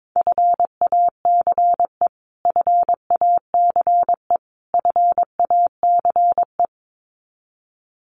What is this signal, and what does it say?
Morse code, heard as a keyed tone, about 22 wpm, 700 Hz. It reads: FACE FACE FACE
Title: Top 500 Words Repeated 22wpm